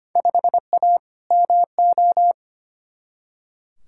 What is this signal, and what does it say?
Morse code, heard as a keyed tone, about 25 wpm, 700 Hz. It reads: 5A MO
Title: Field Day Exchanges Encore - 25wpm